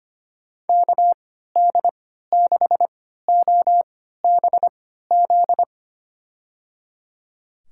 Morse code, heard as a keyed tone, about 25 wpm, spaced as for 14 wpm, 700 Hz. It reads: KD6OBZ
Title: USA Call Signs - 6 Characters - 3x Character Spacing - 25wpm